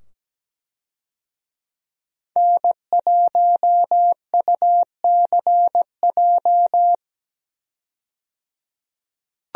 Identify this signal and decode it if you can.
Morse code, heard as a keyed tone, about 17 wpm, 700 Hz. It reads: N1UCJ
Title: USA Call Signs - 5 Characters - 17wpm